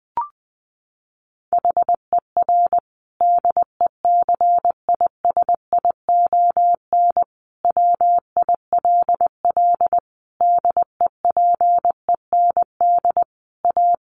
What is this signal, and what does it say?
Morse code, heard as a keyed tone, about 20 wpm, 700 Hz. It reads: HER DECISION WILL DEPEND A
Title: Sentences from Top 1300 Words 20wpm